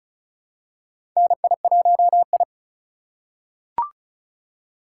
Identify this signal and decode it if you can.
Morse code, heard as a keyed tone, about 36 wpm, 700 Hz. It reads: NI1I
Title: USA Call Signs - 4 Characters - 3x Character Spacing - 35wpm